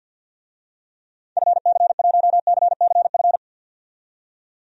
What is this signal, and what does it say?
Morse code, heard as a keyed tone, about 51 wpm, 700 Hz. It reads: FC1CCF
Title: DXCC-100 Call Signs - 6 Characters - 50wpm